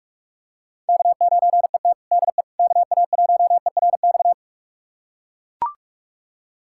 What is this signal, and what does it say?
Morse code, heard as a keyed tone, about 41 wpm, 700 Hz. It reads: K9ET DE KA1ERX
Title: USA Callsign DE Callsign 45wpm